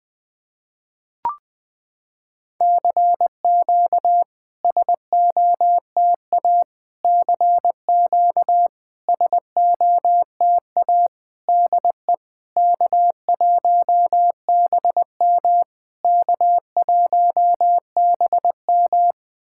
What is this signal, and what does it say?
Morse code, heard as a keyed tone, about 20 wpm, 690 Hz. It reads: CQ SOTA CQ SOTA DE K1BM K1BM
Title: SOTA QSO Practice 20wpm